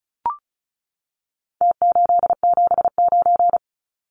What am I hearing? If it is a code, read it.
Morse code, heard as a keyed tone, about 34 wpm, 700 Hz. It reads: T879
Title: Numbers - Contest Style - 4-Digit 35wpm